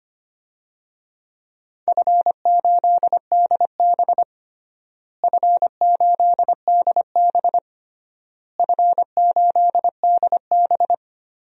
Morse code, heard as a keyed tone, about 25 wpm, 700 Hz. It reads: F8DB F8DB F8DB
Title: DXCC-100 Call Signs - 4 Characters - Repeated 3x - 25wpm